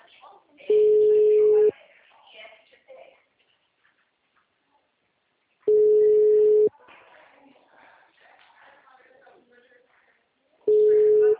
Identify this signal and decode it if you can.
Ringback tone (Europe, 425 Hz), with speech.
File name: out- -2006- - - .31720